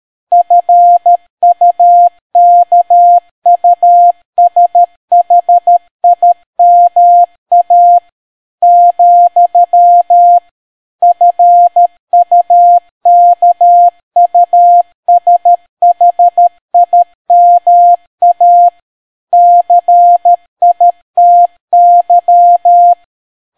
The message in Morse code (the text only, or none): FUKUSHIMA , FUKUSHIMA CITY